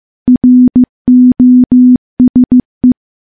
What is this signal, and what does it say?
Morse code, heard as a keyed tone, about 15 wpm, 250 Hz. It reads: ROSE
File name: morse-training-15WPM